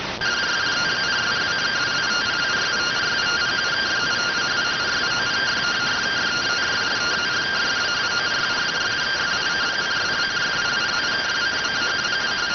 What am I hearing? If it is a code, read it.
Two-tone teleprinter signal, about 170 Hz shift.